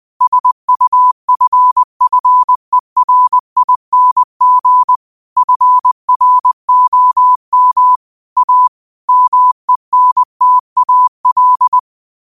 Morse code, heard as a keyed tone, about 20 wpm, 1000 Hz. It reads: SUFFERING FROM A MENTAL
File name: qotdfiles / 20wpm